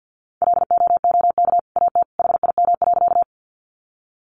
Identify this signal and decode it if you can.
Morse code, heard as a keyed tone, about 50 wpm, 700 Hz. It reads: LOOK AT HIM.
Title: Sentences from Top 100 Words 50wpm